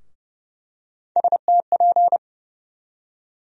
Morse code, heard as a keyed tone, about 30 wpm, 700 Hz. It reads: STP
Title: USA Call Sign Postfixes - 3 Letters 30wpm